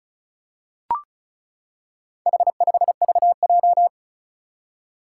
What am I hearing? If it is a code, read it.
Morse code, heard as a keyed tone, about 36 wpm, 700 Hz. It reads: H5VJ
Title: Four Challenging Letter-Number Combinations 35wpm